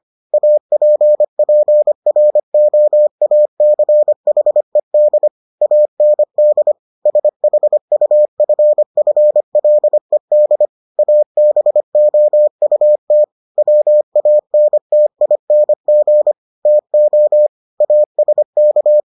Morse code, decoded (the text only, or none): APPROACHED AND SHUFFLED ABOUT WANTING TO ASK